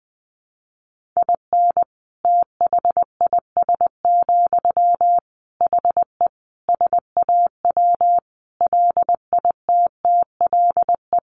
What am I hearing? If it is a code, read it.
Morse code, heard as a keyed tone, about 19 wpm, 700 Hz. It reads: IN THIS, HE SAW LITTLE
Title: Sentences from Top 1500 Words 20wpm